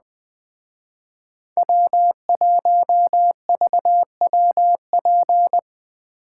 Morse code, heard as a keyed tone, about 20 wpm, 700 Hz. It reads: W1VWP
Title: Speed Racing - USA 1x3 Call Signs 17wpm